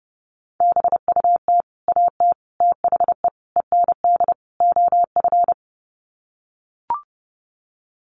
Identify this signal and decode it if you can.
Morse code, heard as a keyed tone, about 30 wpm, 700 Hz. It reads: BUT AT THE END OF
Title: Sets of 5 Words Encore2 30wpm